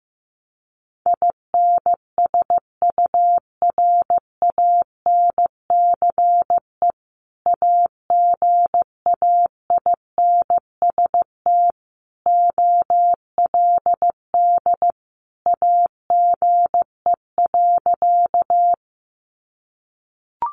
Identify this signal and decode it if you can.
Morse code, heard as a keyed tone, about 15 wpm, 700 Hz. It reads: INSURANCE AGAINST OLD AGE.
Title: Sentences from Top 1100 Words 15wpm